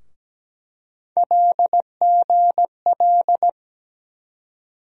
Morse code, heard as a keyed tone, about 17 wpm, 700 Hz. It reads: LGL
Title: USA Call Sign Postfixes - 3 Letters 17wpm